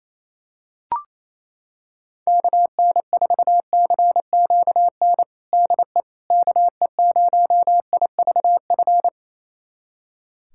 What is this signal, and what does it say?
Morse code, heard as a keyed tone, about 28 wpm, 700 Hz. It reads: KN4CQN DE KE0IVF
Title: USA Callsign DE Callsign - 6 Characters - 28wpm